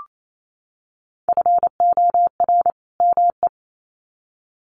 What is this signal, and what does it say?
Morse code, heard as a keyed tone, about 28 wpm, 700 Hz. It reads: FOR ME
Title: Sets of 2 Words 28wpm